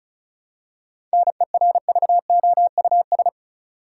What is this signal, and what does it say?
Morse code, heard as a keyed tone, about 34 wpm, 700 Hz. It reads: NERVOUS
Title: Seven Letter Words - 35wpm